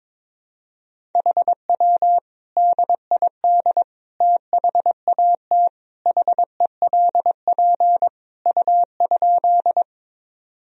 Morse code, heard as a keyed tone, about 22 wpm, 700 Hz. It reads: HW DID THAT HELP U?
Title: Sentences from Top 200 Words and Abbreviations 22wpm